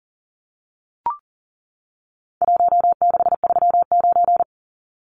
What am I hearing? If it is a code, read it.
Morse code, heard as a keyed tone, about 40 wpm, 700 Hz. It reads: 1639
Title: Numbers - Contest Style - 4-Digit 40wpm